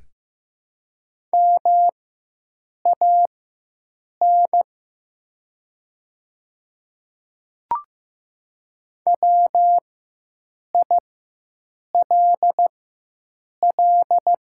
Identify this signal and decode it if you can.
Morse code, heard as a keyed tone, about 15 wpm, spaced as for 7 wpm, 700 Hz. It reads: MAN WILL
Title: 4x Character Spacing - Top 300 Words - 15wpm